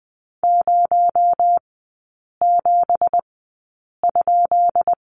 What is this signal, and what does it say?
Morse code, heard as a keyed tone, about 20 wpm, spaced as for 8 wpm, 700 Hz. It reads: 07?
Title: 218 Lesson 35e - Number 0 - Three Characters 20wpm